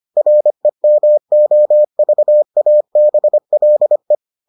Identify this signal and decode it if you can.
Morse code, heard as a keyed tone, about 25 wpm, 600 Hz. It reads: REMOVABLE